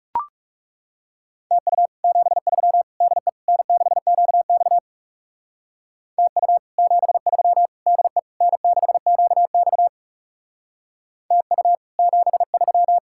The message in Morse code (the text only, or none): TU 73 DE N6QX TU 73 DE N6QX TU 73